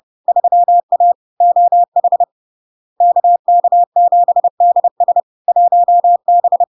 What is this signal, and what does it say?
Morse code, heard as a keyed tone, about 30 wpm, 700 Hz. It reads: 3A OH KK7DS 1B
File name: LARK-FD-2022-Log-in-CW-30wpm-700-Hz